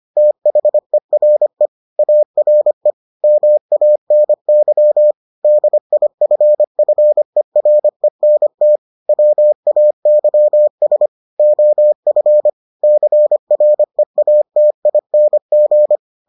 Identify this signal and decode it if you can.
Morse code, heard as a keyed tone, about 25 wpm, 600 Hz. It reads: THERE ARE MANY DIFFERENT WAYS OF CREATING